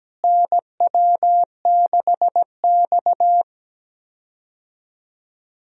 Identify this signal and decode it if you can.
Morse code, heard as a keyed tone, about 17 wpm, 700 Hz. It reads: NW6X
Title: USA Call Signs - 4 Characters - Repeated 2x - 17wpm